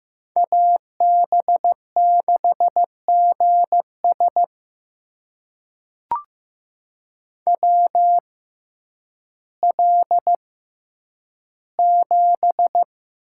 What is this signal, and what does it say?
Morse code, heard as a keyed tone, about 15 wpm, spaced as for 9 wpm, 700 Hz. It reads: AB6GS WL7
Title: USA Call Signs - 6x Character Spacing - 15wpm